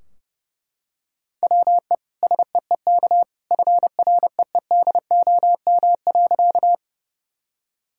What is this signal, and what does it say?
Morse code, heard as a keyed tone, about 30 wpm, 700 Hz. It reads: WE SEEK FREEDOM.
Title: Sentences from Top 1600 Words 30wpm